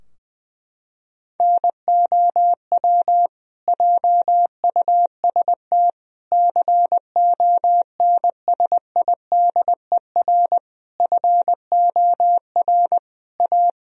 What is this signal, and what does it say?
Morse code, heard as a keyed tone, about 20 wpm, 700 Hz. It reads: NOW JUST CONSIDER FOR A
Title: Sentences from Top 1000 Words - Review Part 4 - 20wpm